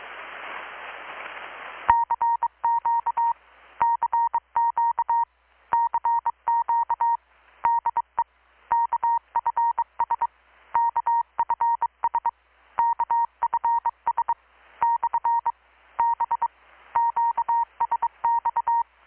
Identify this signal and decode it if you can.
Morse code, heard as a keyed tone, about 23 wpm, 930 Hz. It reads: CQ CQ CQ DE KFS KFS KFS / B QSX
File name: blog / swl / 6-abril-2019-12695-USB-KPS